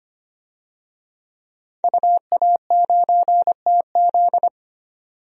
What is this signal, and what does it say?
Morse code, heard as a keyed tone, about 25 wpm, 700 Hz. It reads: UA9TZ
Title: DXCC-100 Call Signs - 5 Characters - 25wpm